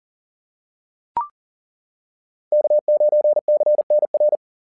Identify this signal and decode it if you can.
Morse code, heard as a keyed tone, about 40 wpm, 600 Hz. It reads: K9CNR
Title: SOTA QSO Practice 40wpm